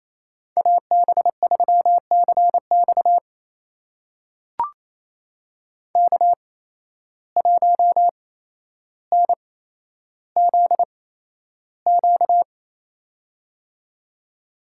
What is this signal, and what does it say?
Morse code, heard as a keyed tone, about 27 wpm, spaced as for 12 wpm, 700 Hz. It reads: AB3CX K1NZQ
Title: USA Call Signs - 5 Characters - 8x Character Spacing - 28wpm